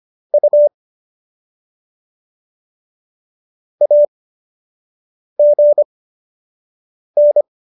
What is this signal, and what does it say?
Morse code, heard as a keyed tone, about 25 wpm, spaced as for 6 wpm, 590 Hz. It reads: U AGN